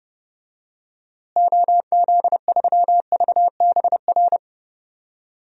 Morse code, heard as a keyed tone, about 30 wpm, 700 Hz. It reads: OZ3VBR
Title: DXCC-100 Call Signs - 6 Characters - Repeated 2x - 30wpm